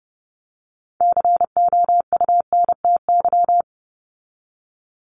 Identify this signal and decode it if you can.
Morse code, heard as a keyed tone, about 30 wpm, 700 Hz. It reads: COUNTY
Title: Top 1001-1250 Words 30wpm